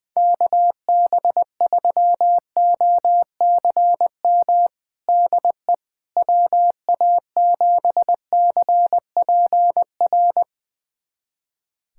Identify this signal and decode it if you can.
Morse code, heard as a keyed tone, about 20 wpm, 700 Hz. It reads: KB3OCM DE WA7CPR
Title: USA Callsign DE Callsign 20wpm